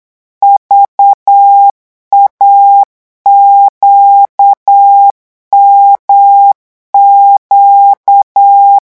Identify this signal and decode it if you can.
Morse code, heard as a keyed tone, about 8 wpm, 800 Hz. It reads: VAQMQ
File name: morse mp3 / revision3-vitesse 8 mots